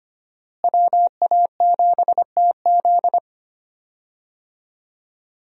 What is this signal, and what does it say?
Morse code, heard as a keyed tone, about 25 wpm, 700 Hz. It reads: WA7TZ
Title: USA Call Signs - 5 Characters - Repeated 2x - 25wpm